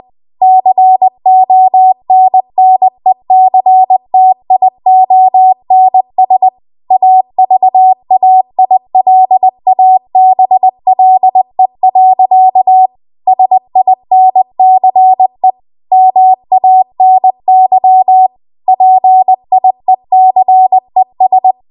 Morse code, heard as a keyed tone, about 20 wpm, 750 Hz. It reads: CONNECTIONS AVAILABLE. SINCE MANY PIECES
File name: Archive / 20 WPM / 20WPM